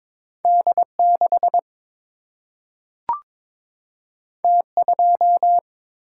Morse code, heard as a keyed tone, about 22 wpm, 700 Hz. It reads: D 6 T 2
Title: Single-Letter Single-Number 22wpm